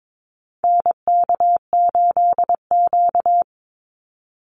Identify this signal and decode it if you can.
Morse code, heard as a keyed tone, about 22 wpm, 700 Hz. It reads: NK8Q
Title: NA SOTA Activator Callsigns 22wpm